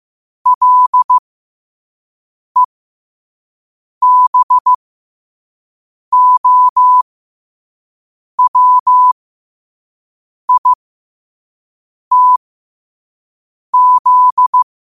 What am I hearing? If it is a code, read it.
Morse code, heard as a keyed tone, about 15 wpm, spaced as for 5 wpm, 1000 Hz. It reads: LEBOWITZ
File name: qotdfiles / 5wpm